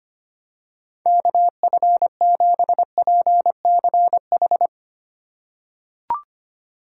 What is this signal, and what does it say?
Morse code, heard as a keyed tone, about 25 wpm, 700 Hz. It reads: KF7PCH
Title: USA Call Signs - 6 Characters - 3x Character Spacing - 25wpm